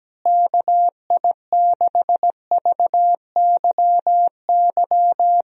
Morse code, heard as a keyed tone, about 17 wpm, 700 Hz. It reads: KI6VYY
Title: USA Call Signs - 6 Characters - Repeated 3x - 17wpm